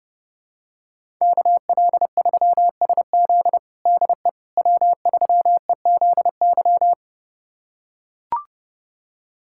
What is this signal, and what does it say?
Morse code, heard as a keyed tone, about 30 wpm, 700 Hz. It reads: KL3SZ DE W3EZY
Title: USA Callsign DE Callsign - 5 Characters - 30wpm